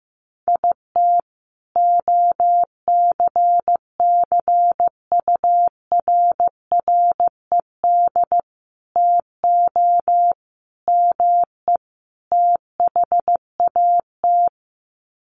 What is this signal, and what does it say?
Morse code, heard as a keyed tone, about 15 wpm, 700 Hz. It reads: IT OCCURRED TO ME THAT
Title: Sets of 5 Words Encore 15wpm